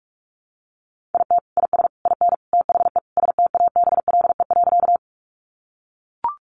Morse code, heard as a keyed tone, about 44 wpm, 700 Hz. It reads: IT IS IN THE STABLE.
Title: Sentences from Top 1500 Words 45wpm